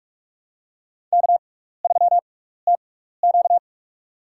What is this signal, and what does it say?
Morse code, heard as a keyed tone, about 46 wpm, spaced as for 15 wpm, 700 Hz. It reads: K3TQ
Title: USA Call Signs - 4 Characters - 6x Character Spacing - 45wpm